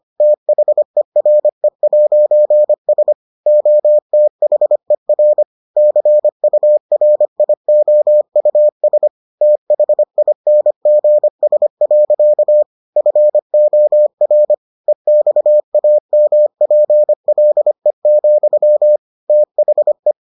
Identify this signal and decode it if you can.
Morse code, heard as a keyed tone, about 25 wpm, 600 Hz. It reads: THERE'S OTHER CURIOUS THINGS. FOR EXAMPLE, THE